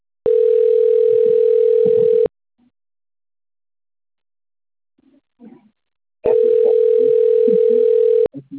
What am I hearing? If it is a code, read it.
Ringback tone (North America), 440 + 480 Hz.